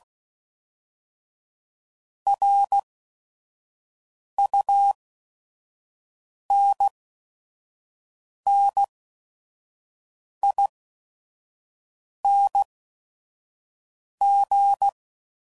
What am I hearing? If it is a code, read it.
Morse code, heard as a keyed tone, about 16 wpm, spaced as for 5 wpm, 800 Hz. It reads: RUNNING
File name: CW Code Groups @5WPM / QSOs / WD0FBT 1